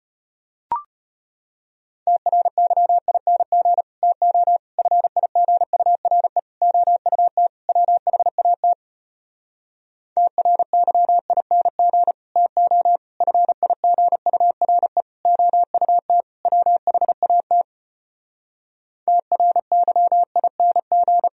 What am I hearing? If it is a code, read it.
Morse code, heard as a keyed tone, about 33 wpm, 700 Hz. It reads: TRYING TO FIGURE OUT WHAT TRYING TO FIGURE OUT WHAT TRYING